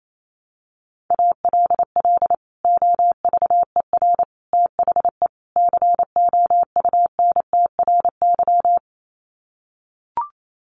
Morse code, heard as a keyed tone, about 28 wpm, 700 Hz. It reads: ALL OVER THE COUNTRY
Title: Sets of 4 Words 28wpm